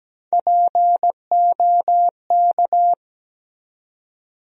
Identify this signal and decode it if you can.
Morse code, heard as a keyed tone, about 17 wpm, 700 Hz. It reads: POK
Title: USA Call Sign Postfixes - 3 Letters 17wpm